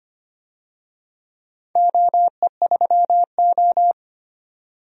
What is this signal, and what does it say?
Morse code, heard as a keyed tone, about 25 wpm, 700 Hz. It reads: OE3O
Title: DXCC-100 Call Signs - 4 Characters - 25wpm